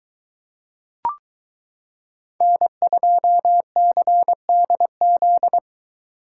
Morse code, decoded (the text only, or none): N2CDZ